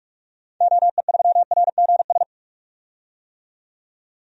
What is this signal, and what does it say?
Morse code, heard as a keyed tone, about 46 wpm, 700 Hz. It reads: OE3RGS
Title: DXCC-100 Call Signs - 6 Characters - 45wpm